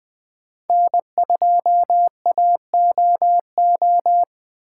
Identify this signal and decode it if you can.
Morse code, heard as a keyed tone, about 20 wpm, 700 Hz. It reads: N2AOO
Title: USA Call Signs - 5 Characters - 4x Character Spacing - 20wpm